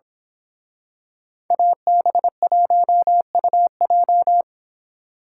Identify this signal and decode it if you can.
Morse code, heard as a keyed tone, about 26 wpm, 700 Hz. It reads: AB1UJ
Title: Speed Racing - USA Call Signs 22wpm